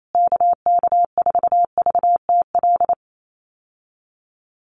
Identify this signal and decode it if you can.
Morse code, heard as a keyed tone, about 28 wpm, 700 Hz. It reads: KK4VTL
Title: USA Call Signs Repeated 2x 28wpm